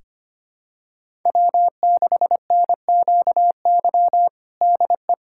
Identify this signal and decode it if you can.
Morse code, heard as a keyed tone, about 25 wpm, 700 Hz. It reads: W6NQY DE
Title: USA Callsign DE Callsign 25wpm